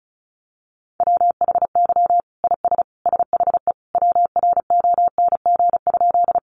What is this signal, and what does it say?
Morse code, heard as a keyed tone, about 34 wpm, 700 Hz. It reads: WHY IS SHE WRONG?
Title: Sentences from Top 900 Words 35wpm